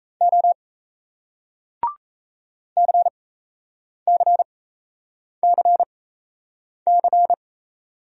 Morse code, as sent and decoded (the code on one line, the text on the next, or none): --- / -.-. -.-. -.-. -.-.
O CCCC